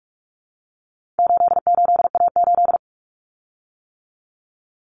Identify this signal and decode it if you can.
Morse code, heard as a keyed tone, about 44 wpm, 700 Hz. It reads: 88A8
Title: Common Cut Numbers - 4-Digit 45wpm